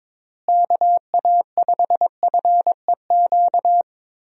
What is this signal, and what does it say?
Morse code, heard as a keyed tone, about 22 wpm, 700 Hz. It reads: KA5FEQ
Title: USA Call Signs - 6 Characters - Repeated 3x - 22wpm